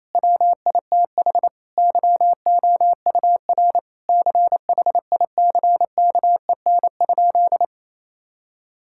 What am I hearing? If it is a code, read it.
Morse code, heard as a keyed tone, about 28 wpm, 700 Hz. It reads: WITH YOUR CHICKEN?
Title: Sentences from Top 1000 Words - Review Part 5 - 28wpm